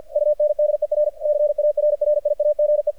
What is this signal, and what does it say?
Morse code, heard as a keyed tone, about 50 wpm, 600 Hz, with radio noise. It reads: UNDERCARRIAGE
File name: archived / CRSnet